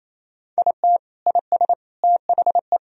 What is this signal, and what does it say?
Morse code, heard as a keyed tone, about 28 wpm, 700 Hz. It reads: IT IS THE
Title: Sentences from Top 1000 Words - Review Part 4 - 28wpm